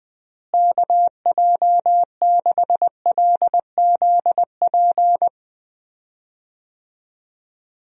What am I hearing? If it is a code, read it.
Morse code, heard as a keyed tone, about 20 wpm, 700 Hz. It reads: KJ6LZP
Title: USA Call Signs - 6 Characters - Repeated 2x - 20wpm